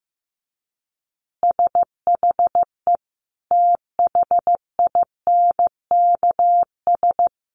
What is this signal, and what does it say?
Morse code, heard as a keyed tone, about 15 wpm, 700 Hz. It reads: SHE THINKS
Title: Easy Phrases 15wpm